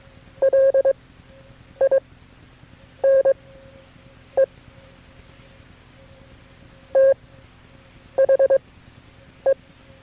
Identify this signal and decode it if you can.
Morse code, heard as a keyed tone, about 22 wpm, spaced as for 8 wpm, 540 Hz. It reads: LINE THE
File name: KB6MT-tape3-side1